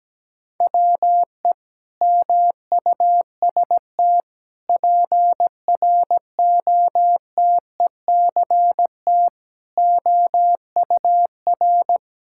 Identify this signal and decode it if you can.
Morse code, heard as a keyed tone, about 17 wpm, 700 Hz. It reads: WE MUST PROTECT OUR
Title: Sentences from Top 1000 Words - Review Part 5 - 17wpm